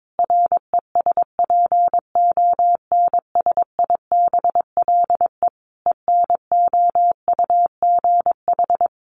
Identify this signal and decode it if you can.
Morse code, heard as a keyed tone, about 22 wpm, 700 Hz. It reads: RESPONSIBLE ENOUGH
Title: Easy Phrases 22wpm